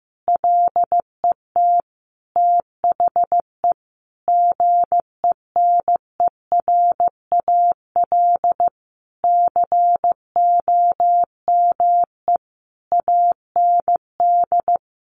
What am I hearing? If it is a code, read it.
Morse code, heard as a keyed tone, about 15 wpm, 700 Hz. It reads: LET THE GENERAL COME AND
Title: Sentences from Top 1400 Words 15wpm